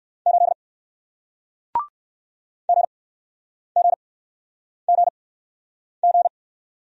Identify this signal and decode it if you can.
Morse code, heard as a keyed tone, about 60 wpm, spaced as for 9 wpm, 700 Hz. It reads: 8 GGGG